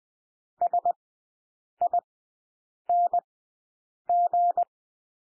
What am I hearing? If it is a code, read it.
Morse code, heard as a keyed tone, about 20 wpm, spaced as for 8 wpm, 700 Hz. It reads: SING
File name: cwmp3 / W1 SOUND WORDS 20 wpm